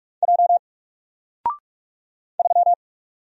Morse code, heard as a keyed tone, about 44 wpm, 700 Hz. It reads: J 3
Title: Single Letter-Number 45wpm